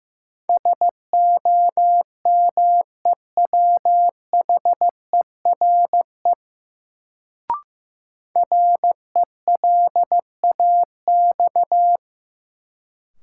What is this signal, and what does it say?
Morse code, heard as a keyed tone, about 15 wpm, 700 Hz. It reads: SOMEWHERE RELAX
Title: Sentences from Top 1400 Words - New Word Drill - 15wpm